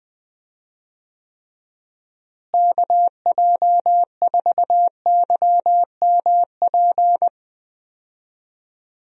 Morse code, heard as a keyed tone, about 20 wpm, 700 Hz. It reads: KJ4YMP
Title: USA Call Signs - 6 Characters - Repeated 3x - 20wpm